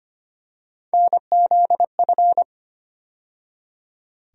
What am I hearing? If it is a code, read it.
Morse code, heard as a keyed tone, about 25 wpm, 700 Hz. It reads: NZF
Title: USA Call Sign Postfixes - 3 Letters 25wpm